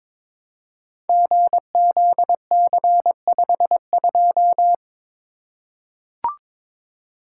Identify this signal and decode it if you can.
Morse code, heard as a keyed tone, about 22 wpm, 700 Hz. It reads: GZC52
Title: Code Groups - 5 Letter:Numbers - 22wpm